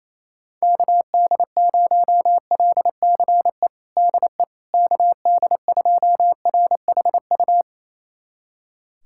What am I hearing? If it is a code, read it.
Morse code, heard as a keyed tone, about 27 wpm, 700 Hz. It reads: KD0LCE DE KD2RHU
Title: USA Callsign DE Callsign - 6 Characters - 28wpm